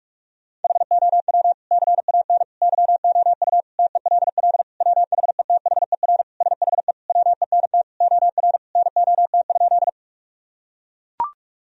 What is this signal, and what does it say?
Morse code, heard as a keyed tone, about 41 wpm, 700 Hz. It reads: HOW CAN YOU TELL WHETHER SHE WENT OR NOT?
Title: Sentences from Top 1000 Words - Review Part 3 - 45wpm